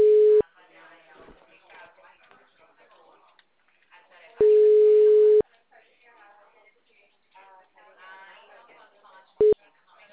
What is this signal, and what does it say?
Ringback tone (Europe, 425 Hz), with speech.
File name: out- -2007- - - .10220